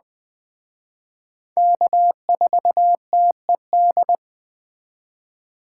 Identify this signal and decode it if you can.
Morse code, heard as a keyed tone, about 20 wpm, 700 Hz. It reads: K4TED